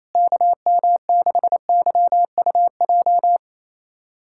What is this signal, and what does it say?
Morse code, heard as a keyed tone, about 28 wpm, 700 Hz. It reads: KM6YUJ
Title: USA Callsign DE Callsign 28wpm